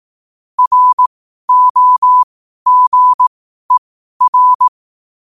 Morse code, heard as a keyed tone, about 18 wpm, spaced as for 12 wpm, 1000 Hz. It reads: ROGER
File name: qotdfiles / 12wpm